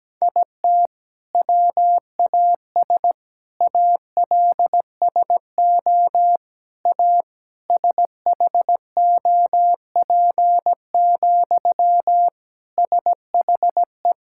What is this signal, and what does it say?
Morse code, heard as a keyed tone, about 17 wpm, 700 Hz. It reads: IT WAS ALSO A SHOP, SHE
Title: Sentences from Top 1600 Words 17wpm